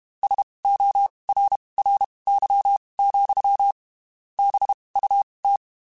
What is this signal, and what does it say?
Morse code, heard as a keyed tone, about 32 wpm, spaced as for 23 wpm, 800 Hz. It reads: SORRY, BUT
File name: (spaced) / QSOs Spaced-mp3s / kc7haa @30 sp